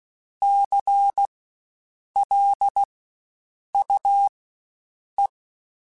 Morse code, heard as a keyed tone, about 16 wpm, spaced as for 7 wpm, 800 Hz. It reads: CLUE
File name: Spaced Alphas / Spaced Alphas mp3s / Cs sp @15